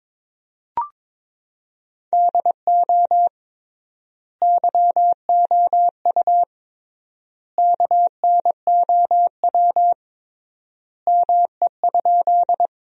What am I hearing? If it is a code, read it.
Morse code, heard as a keyed tone, about 22 wpm, 700 Hz. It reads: DO YOU KNOW ME?
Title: 3x Word-Spacing - Sentences from Top 200 Words - 22wpm